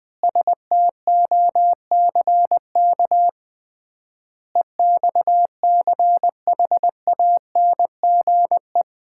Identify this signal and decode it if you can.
Morse code, heard as a keyed tone, about 20 wpm, 700 Hz. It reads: STOCK EXCHANGE
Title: 3x Word-Spacing - Sets of 5 Words - 20wpm